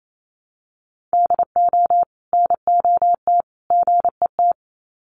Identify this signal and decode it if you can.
Morse code, heard as a keyed tone, about 28 wpm, 700 Hz. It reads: DO NOT GET
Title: Sets of 3 Words 28wpm